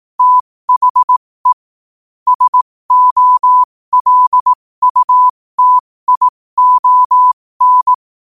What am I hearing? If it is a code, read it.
Morse code, heard as a keyed tone, about 18 wpm, spaced as for 15 wpm, 1000 Hz. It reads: THE SOLUTION
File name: qotdfiles / 15wpm